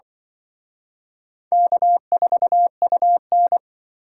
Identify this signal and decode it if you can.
Morse code, heard as a keyed tone, about 24 wpm, 700 Hz. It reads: K4UN
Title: Speed Racing - USA 1x2 Call Signs 20wpm